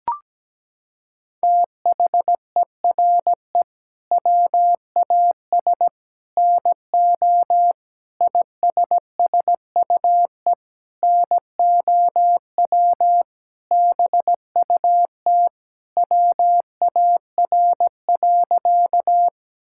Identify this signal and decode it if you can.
Morse code, heard as a keyed tone, about 17 wpm, 700 Hz. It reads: THERE WAS NO ISSUE NOW BUT WAR.
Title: Sentences from Top 1000 Words - Review Part 6 - 17wpm